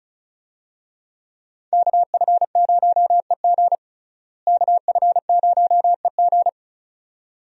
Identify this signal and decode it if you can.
Morse code, heard as a keyed tone, about 34 wpm, 700 Hz. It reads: KF0EG KF0EG
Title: USA Call Signs - 5 Characters - Repeated 2x - 35wpm